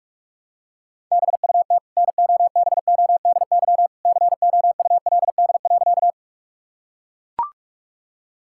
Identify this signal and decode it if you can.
Morse code, heard as a keyed tone, about 41 wpm, 700 Hz. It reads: BUT NOBODY COULD.
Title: Sentences from Top 1600 Words 45wpm